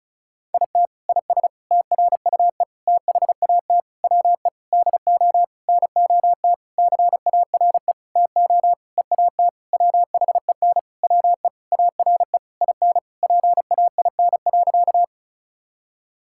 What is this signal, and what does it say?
Morse code, heard as a keyed tone, about 36 wpm, 700 Hz. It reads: IT IS TRUE THAT WE DO NOT CARE TO EAT WHEN WE ARE IN PAIN.
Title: Sentences from Top 1600 Words 35wpm